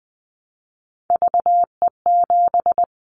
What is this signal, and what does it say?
Morse code, heard as a keyed tone, about 20 wpm, 700 Hz. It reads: VE7
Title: Callsign Prefixes for Top 100 DXCC - 20wpm